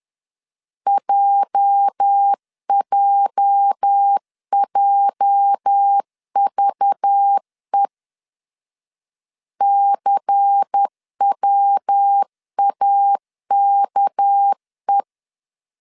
Morse code, heard as a keyed tone, about 10 wpm, 800 Hz. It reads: JJJVE CWAKE
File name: LEZIONI / 8 YJ45 / 8 YJ 10wpm 2651